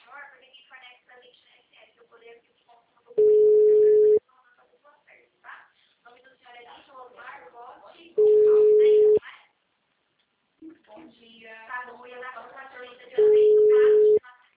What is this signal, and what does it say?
Ringback tone (Europe, 425 Hz), with speech.
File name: out- -2015- - - .5833